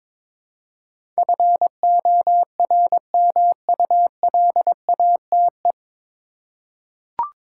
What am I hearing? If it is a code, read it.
Morse code, heard as a keyed tone, about 22 wpm, 700 Hz. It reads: FORMULATE
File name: 9 Letter Words - First 3 Letters Given - 22wpm